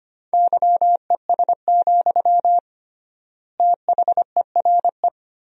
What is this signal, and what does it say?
Morse code, heard as a keyed tone, about 25 wpm, 700 Hz. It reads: YES, THERE
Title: 3x Word-Spacing - Sentences from Top 500 Words - 25wpm